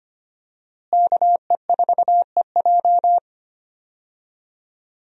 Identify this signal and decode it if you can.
Morse code, heard as a keyed tone, about 25 wpm, 700 Hz. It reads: KE4EJ
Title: USA Call Signs - 5 Characters - Repeated 3x - 25wpm